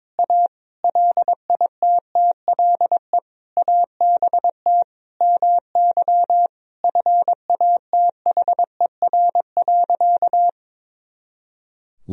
Morse code, heard as a keyed tone, about 22 wpm, 700 Hz. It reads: A LITTLE ABT MY FATHER.
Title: Sentences from Top 400 Words and Abbreviations 22wpm